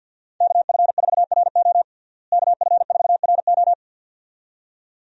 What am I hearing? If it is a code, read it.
Morse code, heard as a keyed tone, about 51 wpm, 700 Hz. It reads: KF4RO KF4RO
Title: USA Call Signs - 5 Characters - Repeated 2x - 50wpm